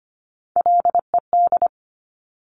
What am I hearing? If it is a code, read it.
Morse code, heard as a keyed tone, about 25 wpm, 700 Hz. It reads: LED
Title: Top Three Letter Combinations in Words 25wpm